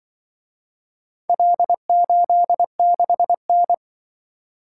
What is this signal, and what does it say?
Morse code, heard as a keyed tone, about 25 wpm, 700 Hz. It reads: L86N
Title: Speed-Racing - Four Letter-Number Combinations - 20wpm